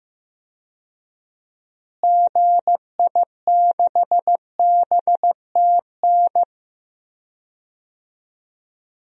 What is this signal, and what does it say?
Morse code, heard as a keyed tone, about 15 wpm, 700 Hz. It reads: GI6BTN
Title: DXCC-100 Call Signs - 6 Characters - Repeated 3x - 15wpm